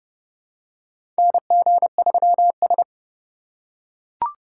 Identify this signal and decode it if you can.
Morse code, heard as a keyed tone, about 30 wpm, 700 Hz. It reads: NG3S
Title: USA Call Signs - 4 Characters - 4x Character Spacing - 30wpm